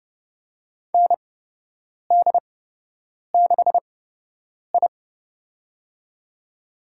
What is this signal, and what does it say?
Morse code, heard as a keyed tone, about 30 wpm, spaced as for 8 wpm, 700 Hz. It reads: ND6I
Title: USA Call Signs - 4 Characters - 8x Character Spacing - 30wpm